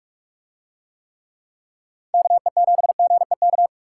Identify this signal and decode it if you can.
Morse code, heard as a keyed tone, about 44 wpm, 700 Hz. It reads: KE7GEK